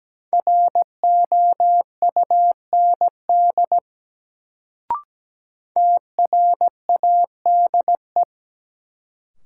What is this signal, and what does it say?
Morse code, heard as a keyed tone, about 17 wpm, 700 Hz. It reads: ROUND TRADE
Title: Sets of 5 Words Encore - New Word Drill - 17wpm